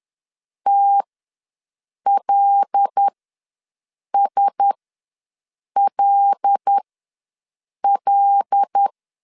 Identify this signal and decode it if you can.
Morse code, heard as a keyed tone, about 10 wpm, spaced as for 6 wpm, 800 Hz. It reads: TLSLL
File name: LEZIONI / 5 LBVC / 5 L ws 10wpm 3636